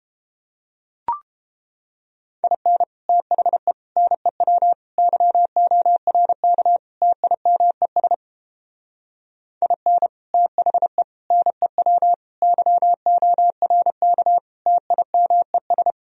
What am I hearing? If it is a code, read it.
Morse code, heard as a keyed tone, about 29 wpm, 700 Hz. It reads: IN THE NEW YORK TIMES IN THE NEW YORK TIMES